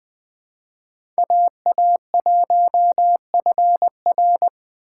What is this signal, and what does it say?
Morse code, heard as a keyed tone, about 20 wpm, 700 Hz. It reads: AA1FR
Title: USA Call Signs - 5 Characters - 4x Character Spacing - 20wpm